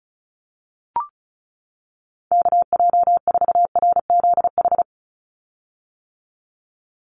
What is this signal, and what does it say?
Morse code, heard as a keyed tone, about 34 wpm, 700 Hz. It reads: KJ4RZH
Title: USA Call Signs 35wpm